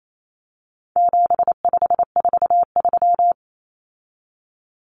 Morse code, heard as a keyed tone, about 28 wpm, 700 Hz. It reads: 7543
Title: Common Cut Numbers - 4-Digit 28wpm